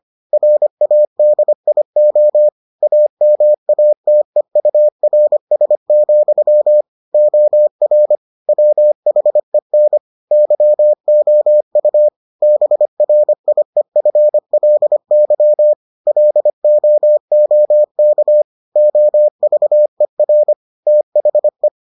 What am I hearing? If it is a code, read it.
Morse code, heard as a keyed tone, about 25 wpm, 600 Hz. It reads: RADIO AMATEURS, OR WHEN YOU BRIEFLY LOOK OVER THE